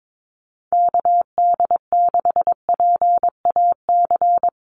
Morse code, heard as a keyed tone, about 22 wpm, 700 Hz. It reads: KD6PAC
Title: USA Call Signs 22wpm